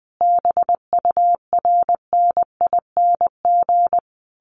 Morse code, heard as a keyed tone, about 20 wpm, 700 Hz. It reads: BURNING
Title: A Sound Of Thunder 20wpm